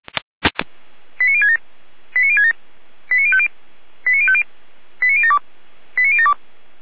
Five-tone selective call (ZVEI), heard as quick runs of five tones, tones of about 70 ms. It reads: 89067 89067 89050 89050 89072 89072